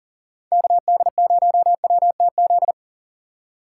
Morse code, heard as a keyed tone, about 40 wpm, 690 Hz. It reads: KD0WTZ
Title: USA Call Signs - 6 Characters - 2x Character Spacing - 40wpm